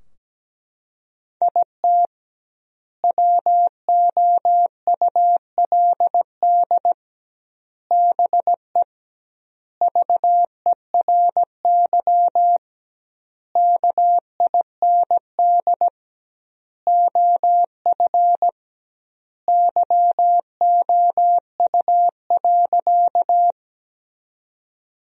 Morse code, heard as a keyed tone, about 17 wpm, 700 Hz. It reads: IT WOULD BE VERY KIND OF YOU.
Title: 2x Word-Spacing - Sentences from Top 200 Words - 17wpm